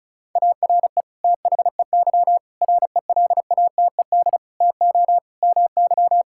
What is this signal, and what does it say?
Morse code, heard as a keyed tone, about 36 wpm, 690 Hz. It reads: ARE THEY RELATED TO MY
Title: Sentences from Top 1000 Words - Review Part 5 - 35wpm